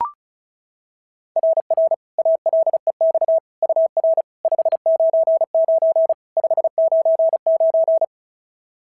Morse code, heard as a keyed tone, about 33 wpm, 650 Hz. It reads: RR ALEX UR 599 599
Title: SOTA QSO Practice 35wpm